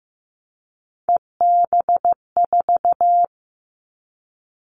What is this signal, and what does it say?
Morse code, heard as a keyed tone, about 15 wpm, 700 Hz. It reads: EB4
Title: Callsign Prefixes for Top 25 DXCC - 15wpm